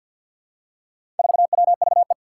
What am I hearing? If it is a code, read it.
Morse code, heard as a keyed tone, about 52 wpm, 700 Hz. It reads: 4WVE
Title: Four Challenging Letter-Number Combinations 50wpm